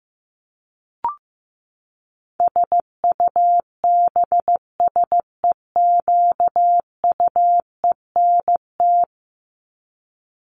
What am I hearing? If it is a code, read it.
Morse code, heard as a keyed tone, about 15 wpm, 700 Hz. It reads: SUBSEQUENT
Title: LY Suffix - Top 100 Words with Base Words 15wpm